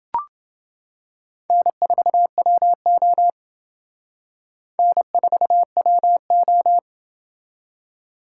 Morse code, heard as a keyed tone, about 28 wpm, 700 Hz. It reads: N4WO N4WO